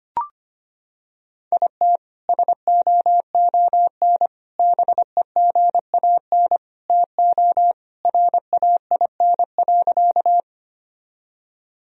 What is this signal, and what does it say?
Morse code, heard as a keyed tone, about 25 wpm, 700 Hz. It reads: IT SOON BEGAN TO RAIN.
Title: Sentences from Top 1000 Words - Review Part 1 - 25wpm